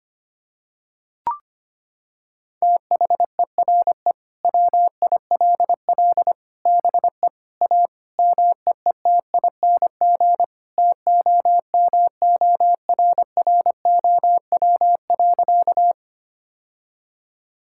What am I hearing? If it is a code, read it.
Morse code, heard as a keyed tone, about 25 wpm, 700 Hz. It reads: THERE WILL BE A MEETING TOMORROW.
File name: practice / 25wpm / Sentences with 4-Word Phrases - 25wpm